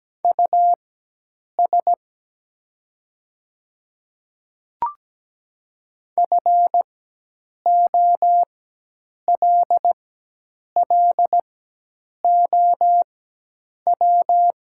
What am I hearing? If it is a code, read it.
Morse code, heard as a keyed tone, about 17 wpm, spaced as for 8 wpm, 690 Hz. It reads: US FOLLOW
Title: 4x Character Spacing - Top 200 Words - 17wpm